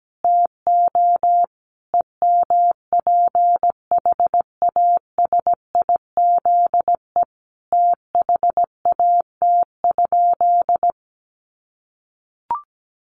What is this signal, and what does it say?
Morse code, heard as a keyed tone, about 17 wpm, 700 Hz. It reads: TO EMPHASIZE THAT?
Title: Sentences from Top 1500 Words 17wpm